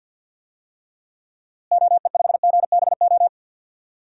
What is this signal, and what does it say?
Morse code, heard as a keyed tone, about 47 wpm, 700 Hz. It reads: OE5GBO
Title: DXCC-100 Call Signs - 6 Characters - Repeated 2x - 50wpm